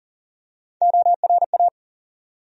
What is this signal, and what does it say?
Morse code, heard as a keyed tone, about 40 wpm, 700 Hz. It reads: ORA
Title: USA Call Sign Postfixes - 3 Letters 40wpm